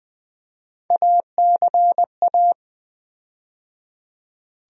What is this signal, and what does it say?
Morse code, heard as a keyed tone, about 20 wpm, 700 Hz. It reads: ACA
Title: USA Call Sign Postfixes - 3 Letters 20wpm